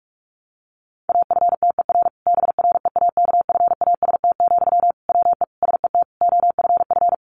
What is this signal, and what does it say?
Morse code, heard as a keyed tone, about 44 wpm, 700 Hz. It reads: AFTER BREAKFAST, WE SET OFF
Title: Sentences from Top 1500 Words 45wpm